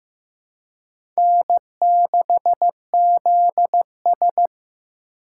Morse code, heard as a keyed tone, about 15 wpm, 700 Hz. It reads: N6ZS
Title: USA Call Signs - 4 Characters - 3x Character Spacing - 15wpm